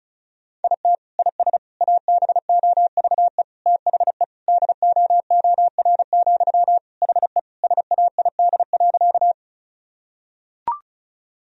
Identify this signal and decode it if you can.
Morse code, heard as a keyed tone, about 34 wpm, 700 Hz. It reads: IT IS ABOVE THE DOOR, HE SAID.
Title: Sentences from Top 1000 Words - Review Part 2 - 35wpm